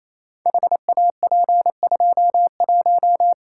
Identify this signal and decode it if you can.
Morse code, heard as a keyed tone, about 28 wpm, 700 Hz. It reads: HAP21
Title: Code Groups - 5 Letter:Numbers - 28wpm